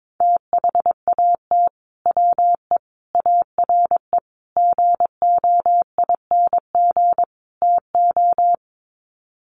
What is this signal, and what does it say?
Morse code, heard as a keyed tone, about 22 wpm, 700 Hz. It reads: THAT WE ARE GOING TO
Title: Sets of 5 Words Encore 22wpm